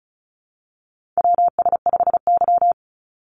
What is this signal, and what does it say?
Morse code, heard as a keyed tone, about 36 wpm, 700 Hz. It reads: WS5Y
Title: USA Call Signs 35wpm